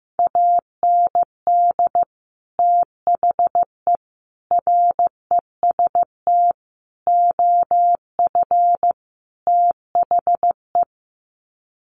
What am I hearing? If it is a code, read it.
Morse code, heard as a keyed tone, about 15 wpm, 700 Hz. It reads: AND THE REST OF THE
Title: Sets of 5 Words 15wpm